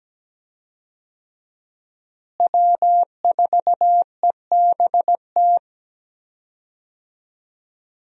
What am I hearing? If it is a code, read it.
Morse code, heard as a keyed tone, about 17 wpm, 700 Hz. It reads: W4EBT
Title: USA Call Signs - 5 Characters - Repeated 3x - 17wpm